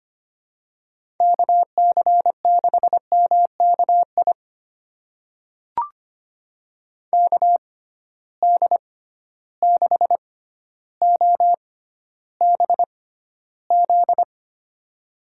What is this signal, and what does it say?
Morse code, heard as a keyed tone, about 25 wpm, spaced as for 13 wpm, 700 Hz. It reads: KC6MKI KD6OBZ
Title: USA Call Signs - 6x Character Spacing - 25wpm